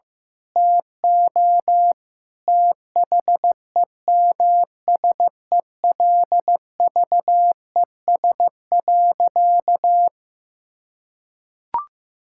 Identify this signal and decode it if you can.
Morse code, heard as a keyed tone, about 15 wpm, 700 Hz. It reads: TO THEMSELVES.
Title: Sentences from Top 1000 Words - Review Part 6 - 15wpm